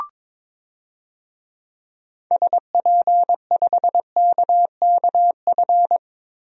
Morse code, heard as a keyed tone, about 22 wpm, 700 Hz. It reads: SP5KKF
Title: DXCC-100 Call Signs - 6 Characters - 22wpm